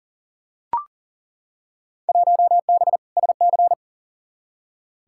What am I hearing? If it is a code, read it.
Morse code, heard as a keyed tone, about 40 wpm, 700 Hz. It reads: 1B SC
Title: Field Day Exchanges Encore - 40wpm